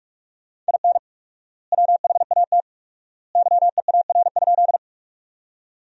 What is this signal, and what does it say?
Morse code, heard as a keyed tone, about 46 wpm, 700 Hz. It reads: IN WHAT YEAR?
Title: 4x Word-Spacing - Sentences from Top 200 Words - 45wpm